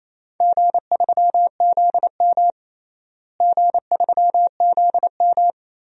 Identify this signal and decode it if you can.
Morse code, heard as a keyed tone, about 27 wpm, 700 Hz. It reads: G3ZM G3ZM
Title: DXCC-100 Call Signs - 4 Characters - Repeated 3x - 28wpm